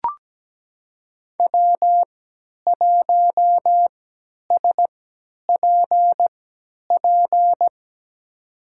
Morse code, heard as a keyed tone, about 17 wpm, spaced as for 10 wpm, 700 Hz. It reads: W1SPP
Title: USA Call Signs - 5 Characters - 3x Character Spacing - 17wpm